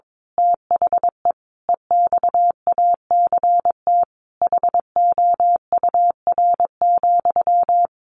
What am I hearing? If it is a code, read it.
Morse code, heard as a keyed tone, about 22 wpm, 700 Hz. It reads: THE EXACT HOUR,
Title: A Sound Of Thunder 22wpm